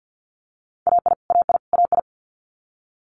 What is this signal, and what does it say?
Morse code, heard as a keyed tone, about 50 wpm, 700 Hz. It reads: AI AI AI
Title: Multi-letter Phonemes Repeated 3x 50wpm